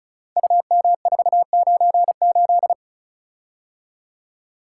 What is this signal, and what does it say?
Morse code, heard as a keyed tone, about 36 wpm, 700 Hz. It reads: UM498
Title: Code Groups - 5 Letter:Numbers - 35wpm